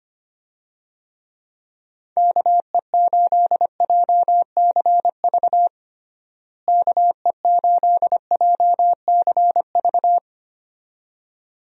Morse code, heard as a keyed tone, about 25 wpm, 700 Hz. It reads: KE8JCV KE8JCV
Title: USA Call Signs - 6 Characters - Repeated 2x - 25wpm